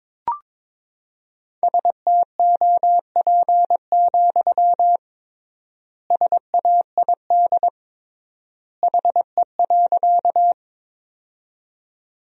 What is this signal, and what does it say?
Morse code, heard as a keyed tone, about 21 wpm, 700 Hz. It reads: STOP, SAID HE.
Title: 3x Word-Spacing - Sentences from Top 300 Words - 22wpm